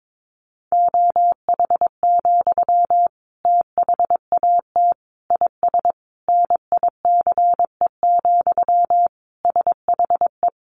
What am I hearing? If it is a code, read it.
Morse code, heard as a keyed tone, about 22 wpm, 700 Hz. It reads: OH, THAT IS NICE, SHE
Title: Sentences from Top 1100 Words 22wpm